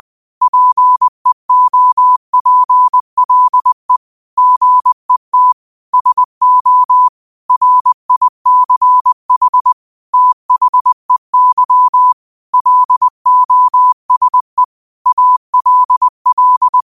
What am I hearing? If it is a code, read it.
Morse code, heard as a keyed tone, about 20 wpm, 1000 Hz. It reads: PEOPLE GET SO RICH THEY LOSE ALL
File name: 20wpm